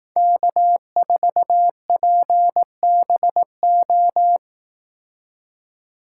Morse code, heard as a keyed tone, about 18 wpm, 700 Hz. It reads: K4PBO
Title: Speed Racing - USA 1x3 Call Signs 15wpm